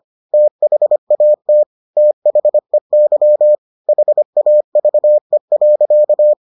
Morse code, heard as a keyed tone, about 25 wpm, 600 Hz. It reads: THAT THEY HAVE.